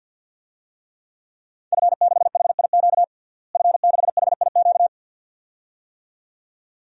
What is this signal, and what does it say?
Morse code, heard as a keyed tone, about 46 wpm, 700 Hz. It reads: F6HIQ F6HIQ
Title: DXCC-100 Call Signs - 5 Characters - Repeated 2x - 50wpm